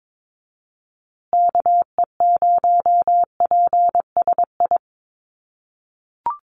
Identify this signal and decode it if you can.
Morse code, heard as a keyed tone, about 22 wpm, 700 Hz. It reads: KE0PSI
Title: USA Call Signs Repeated 2x 22wpm